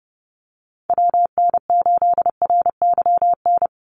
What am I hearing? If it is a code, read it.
Morse code, heard as a keyed tone, about 31 wpm, 700 Hz. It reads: WN8RYN
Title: USA Call Signs 30wpm